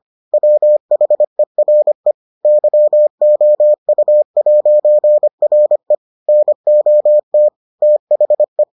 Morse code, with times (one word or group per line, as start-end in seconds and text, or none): 0.3-2.1 WHERE
2.5-6.0 YOU'RE
6.3-7.5 NOT
7.8-8.6 THE